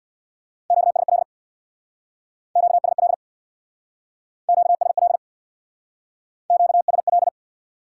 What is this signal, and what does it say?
Morse code, heard as a keyed tone, about 63 wpm, 700 Hz. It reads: QSL QSL QSL QSL